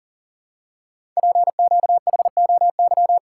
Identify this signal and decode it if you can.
Morse code, heard as a keyed tone, about 42 wpm, 690 Hz. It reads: PQHOY
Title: Code Groups - 5 Letters - 40wpm